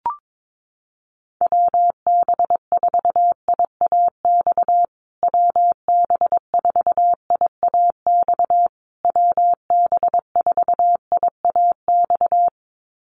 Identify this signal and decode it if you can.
Morse code, heard as a keyed tone, about 22 wpm, 700 Hz. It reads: WB4IAX WB4IAX WB4IAX
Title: USA Call Signs Repeated 3x 22wpm